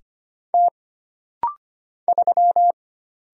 Morse code, heard as a keyed tone, about 25 wpm, 700 Hz. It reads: T 3
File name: Single Character (Letter:Number:Punctuation:Prosign) - Rapid-Fire - 25wpm